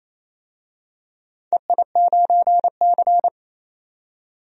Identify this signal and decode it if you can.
Morse code, heard as a keyed tone, about 28 wpm, 700 Hz. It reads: EI9C
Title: DXCC-100 Call Signs - 4 Characters - 28wpm